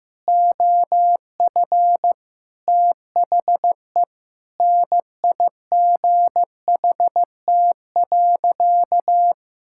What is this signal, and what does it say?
Morse code, heard as a keyed tone, about 15 wpm, 700 Hz. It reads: OF THE NIGHT.
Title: Sentences from Top 1000 Words - Review Part 6 - 15wpm